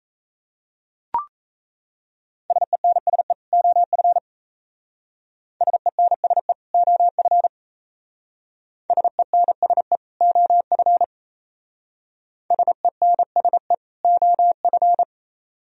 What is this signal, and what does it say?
Morse code, heard as a keyed tone, about 36 wpm, 690 Hz. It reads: SENSE OF SENSE OF SENSE OF SENSE OF